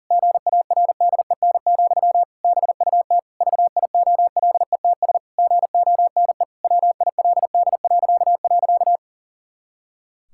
Morse code, heard as a keyed tone, about 40 wpm, 700 Hz. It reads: GARDEN, BUT VIOLETS GONE WILD..
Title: Pangram Challenge v2 - 40wpm